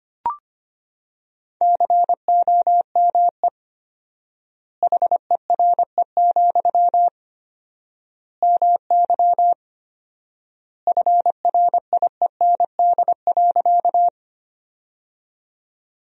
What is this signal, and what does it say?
Morse code, heard as a keyed tone, about 25 wpm, 700 Hz. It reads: COME HERE, MY FRIEND.
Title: 4x Word-Spacing - Sentences from Top 300 Words - 25wpm